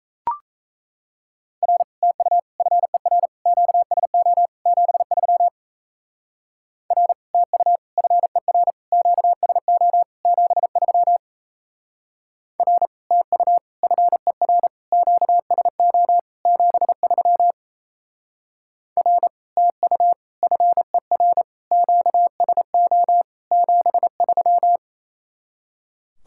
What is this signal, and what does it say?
Morse code, heard as a keyed tone, about 34 wpm, 700 Hz. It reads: R TU FER QSO 73 R TU FER QSO 73 R TU FER QSO 73 R TU FER QSO 73
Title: Speed-Racing Common QSO Phrases - 28wpm